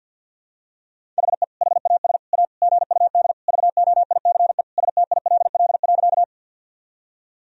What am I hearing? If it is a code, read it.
Morse code, heard as a keyed tone, about 47 wpm, 700 Hz. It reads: HE HAS A GUD VOICE STILL.
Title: Sentences from Top 500 Words and Abbreviations 50wpm